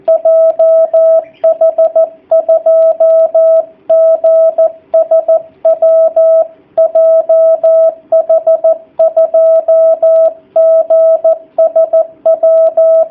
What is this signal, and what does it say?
Morse code, heard as a keyed tone, about 14 wpm, 660 Hz. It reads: JH2GSW JH2GSW
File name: jh8cbh / 2019cw / sp13- w